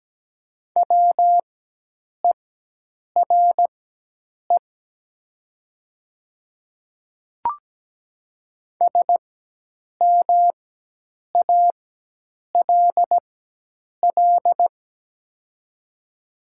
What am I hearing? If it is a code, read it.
Morse code, heard as a keyed tone, about 17 wpm, spaced as for 8 wpm, 700 Hz. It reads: WERE SMALL
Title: 4x Character Spacing - Top 200 Words - 17wpm